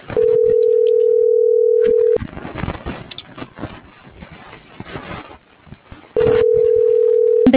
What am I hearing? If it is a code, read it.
Ringback tone (North America), 440 + 480 Hz.
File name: CENTRAL TERRESTRE / out vane.lazaro